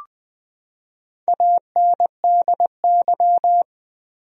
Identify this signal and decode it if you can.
Morse code, heard as a keyed tone, about 20 wpm, 700 Hz. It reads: ANDY
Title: USA Common Names - Max 4 Letters - 20wpm